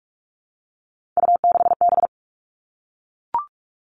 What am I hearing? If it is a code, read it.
Morse code, heard as a keyed tone, about 44 wpm, 700 Hz. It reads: U6B
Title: All Cut Numbers - 3-Digit 45wpm